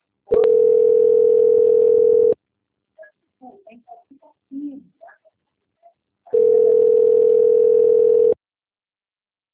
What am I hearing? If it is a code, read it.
Ringback tone (North America), 440 + 480 Hz, with speech.
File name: out- - - - - .14062